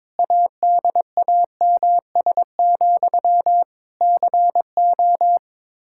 Morse code, heard as a keyed tone, about 22 wpm, 700 Hz. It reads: ADAMS, CO
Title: Practice QSO - Exchange 1 - Beginning - 22wpm